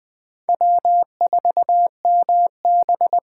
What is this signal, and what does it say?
Morse code, heard as a keyed tone, about 20 wpm, 700 Hz. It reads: W4MB
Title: USA Callsign DE Callsign - 4 Characters - 20wpm